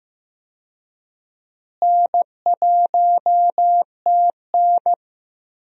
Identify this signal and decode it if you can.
Morse code, heard as a keyed tone, about 15 wpm, 700 Hz. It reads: N1TN
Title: USA Call Signs - 4 Characters - Repeated 2x - 15wpm